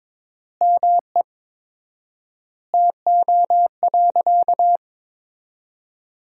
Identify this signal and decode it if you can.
Morse code, heard as a keyed tone, about 22 wpm, 700 Hz. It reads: ME TO.
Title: 4x Word-Spacing - Sentences from Top 400 Words - 22wpm